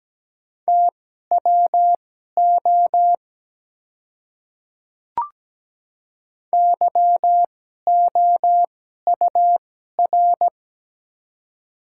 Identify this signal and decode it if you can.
Morse code, heard as a keyed tone, about 17 wpm, spaced as for 12 wpm, 700 Hz. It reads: TWO YOUR
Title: 2x Character Spacing - Top 100 Words - 17wpm